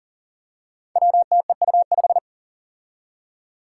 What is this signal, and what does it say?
Morse code, heard as a keyed tone, about 40 wpm, 700 Hz. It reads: WTEU5
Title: Five Challenging Letter-Number Combinations 40wpm